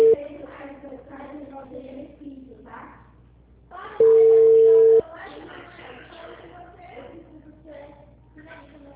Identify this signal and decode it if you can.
Ringback tone (China, 450 Hz), with speech.
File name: out- -2011- - - .47201